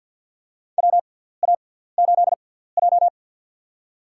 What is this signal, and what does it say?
Morse code, heard as a keyed tone, about 53 wpm, spaced as for 17 wpm, 700 Hz. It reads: WA8J
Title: USA Call Signs - 4 Characters - 6x Character Spacing - 50wpm